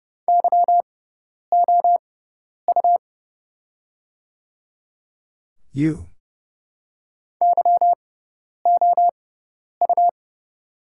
Morse code, heard as a keyed tone, about 30 wpm, spaced as for 10 wpm, 690 Hz. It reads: YOU YOU
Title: 6x Character Spacing - Top 500 Words - 30wpm